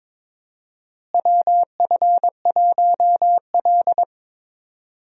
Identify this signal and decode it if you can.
Morse code, heard as a keyed tone, about 21 wpm, 700 Hz. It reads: WF1L
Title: USA Call Signs - 4 Characters - Repeated 3x - 22wpm